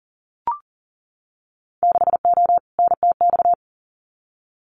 Morse code, heard as a keyed tone, about 40 wpm, 700 Hz. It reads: 6O NTX
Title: Winter Field Day Exchanges Encore 40wpm